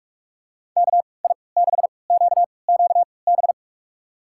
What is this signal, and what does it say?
Morse code, heard as a keyed tone, about 47 wpm, spaced as for 26 wpm, 700 Hz. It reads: KI6QQB
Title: USA Call Signs - 6 Characters - 3x Character Spacing - 45wpm